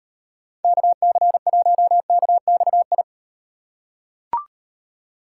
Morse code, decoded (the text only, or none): KC1KXI